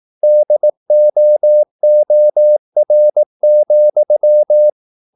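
Morse code, heard as a keyed tone, about 18 wpm, 600 Hz. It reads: DOOR,
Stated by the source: Morse practice at 18 wpm, English text, from CW med Gnister.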